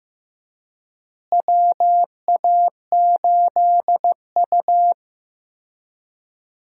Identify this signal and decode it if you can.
Morse code, heard as a keyed tone, about 15 wpm, 700 Hz. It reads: WA8U
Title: USA Call Signs - 4 Characters - Repeated 2x - 15wpm